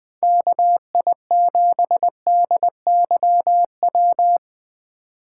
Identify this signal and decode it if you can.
Morse code, heard as a keyed tone, about 20 wpm, 700 Hz. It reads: KI7DYW
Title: USA Call Signs - 6 Characters - 8x Character Spacing - 20wpm